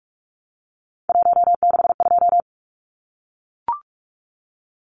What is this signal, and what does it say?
Morse code, heard as a keyed tone, about 44 wpm, 700 Hz. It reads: 162
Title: Numbers - 3-Digit 45wpm